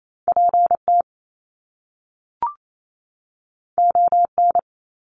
Morse code, heard as a keyed tone, about 28 wpm, 700 Hz. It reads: PT ON
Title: Top Two Letter Combinations in Words 28wpm